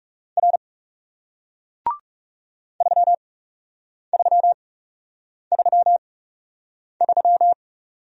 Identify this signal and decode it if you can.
Morse code, heard as a keyed tone, about 39 wpm, spaced as for 8 wpm, 700 Hz. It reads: R 3333